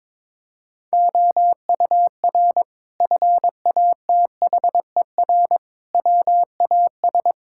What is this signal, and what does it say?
Morse code, heard as a keyed tone, about 21 wpm, 700 Hz. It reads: OUR FATHER WAS
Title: Sentences from Top 1600 Words 22wpm